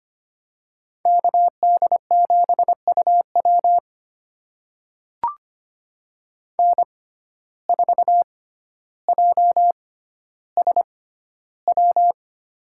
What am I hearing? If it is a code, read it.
Morse code, heard as a keyed tone, about 25 wpm, spaced as for 13 wpm, 700 Hz. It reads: KD7UW N4JSW
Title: USA Call Signs - 5 Characters - 6x Character Spacing - 25wpm